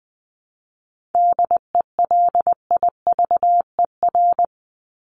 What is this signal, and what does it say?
Morse code, heard as a keyed tone, about 20 wpm, 700 Hz. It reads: DELIVER
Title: Top 1751-2000 Words 20wpm